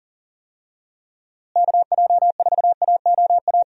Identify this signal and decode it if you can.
Morse code, heard as a keyed tone, about 40 wpm, 700 Hz. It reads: KJ4AOA
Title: USA Call Signs - 6 Characters - Repeated 2x - 40wpm